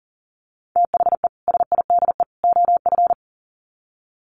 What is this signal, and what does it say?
Morse code, heard as a keyed tone, about 40 wpm, 700 Hz. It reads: THE SIDE OF
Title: Sets of 3 Words 40wpm